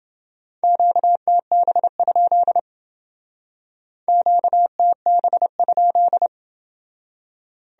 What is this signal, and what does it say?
Morse code, heard as a keyed tone, about 28 wpm, 700 Hz. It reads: QTB? QTB?